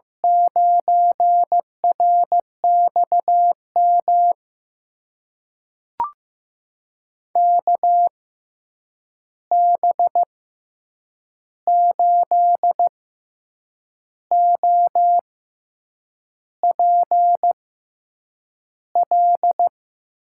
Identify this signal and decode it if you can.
Morse code, heard as a keyed tone, about 15 wpm, spaced as for 7 wpm, 700 Hz. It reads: 9RXM KB8OPL
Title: USA Call Signs - 6 Characters - 6x Character Spacing - 15wpm